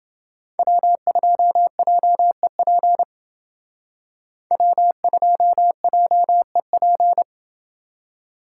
Morse code, heard as a keyed tone, about 28 wpm, 700 Hz. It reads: W2JEP W2JEP